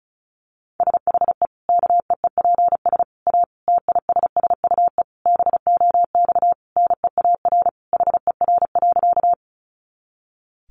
Morse code, heard as a keyed tone, about 34 wpm, 700 Hz. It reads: SHE KEEPS A TISSUE BOX NEAR HER.
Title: Easy Phrases 35wpm